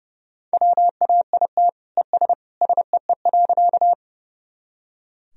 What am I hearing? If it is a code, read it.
Morse code, heard as a keyed tone, about 30 wpm, 700 Hz. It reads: WAIT ES SEE.
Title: Sentences from Top 500 Words and Abbreviations 30wpm